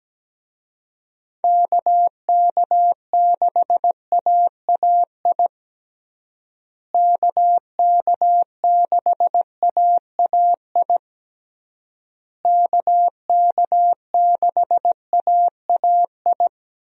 Morse code, heard as a keyed tone, about 17 wpm, 700 Hz. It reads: KK6AAI KK6AAI KK6AAI
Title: USA Call Signs - 6 Characters - Repeated 3x - 17wpm